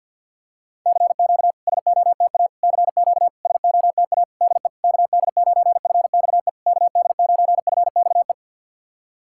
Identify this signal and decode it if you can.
Morse code, heard as a keyed tone, about 50 wpm, 690 Hz. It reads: CQ SOTA CQ SOTA DE KD9FXE KD9FXE
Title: SOTA QSO Practice 50wpm